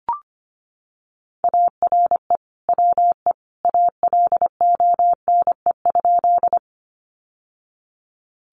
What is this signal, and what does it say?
Morse code, heard as a keyed tone, about 25 wpm, 700 Hz. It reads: ARE WE ALONE?
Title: Sentences from Top 1000 Words 25wpm